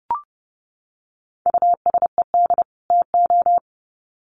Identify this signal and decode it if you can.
Morse code, heard as a keyed tone, about 30 wpm, 700 Hz. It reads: USED TO
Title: Sets of 2 Words 30wpm